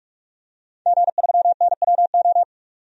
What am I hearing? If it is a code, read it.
Morse code, heard as a keyed tone, about 42 wpm, 700 Hz. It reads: G3NWO
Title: DXCC-100 Call Signs - 5 Characters - 45wpm